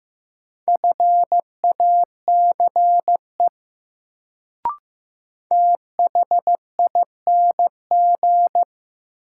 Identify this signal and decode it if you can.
Morse code, heard as a keyed tone, about 15 wpm, 700 Hz. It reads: FACE THING
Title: Sets of 3 Words - New Word Drill - 15wpm